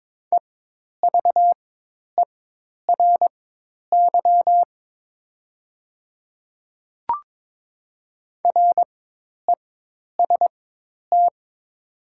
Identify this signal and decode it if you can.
Morse code, heard as a keyed tone, about 22 wpm, spaced as for 10 wpm, 700 Hz. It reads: EVERY REST
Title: 4x Character Spacing - Top 500 Words - 22wpm